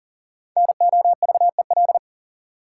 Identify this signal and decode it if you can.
Morse code, heard as a keyed tone, about 40 wpm, 700 Hz. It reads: NOVEL
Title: Sentences from Top 1300 Words - New Word Drill Alt - 40wpm